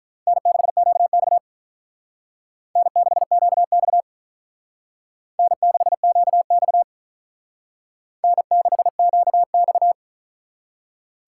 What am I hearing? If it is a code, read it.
Morse code, heard as a keyed tone, about 44 wpm, 700 Hz. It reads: N6QX N6QX N6QX N6QX